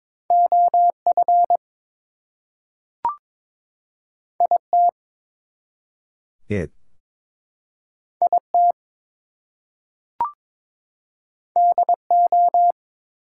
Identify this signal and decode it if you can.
Morse code, heard as a keyed tone, about 22 wpm, 700 Hz. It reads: OF IT IT DO
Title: Two Letter Words - 22wpm